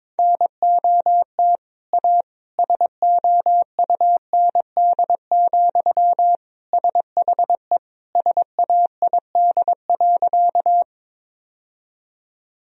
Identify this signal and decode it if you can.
Morse code, heard as a keyed tone, about 22 wpm, 700 Hz. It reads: NOT A SOUND, SHE SAID.
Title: Sentences from Top 1000 Words - Review Part 1 - 22wpm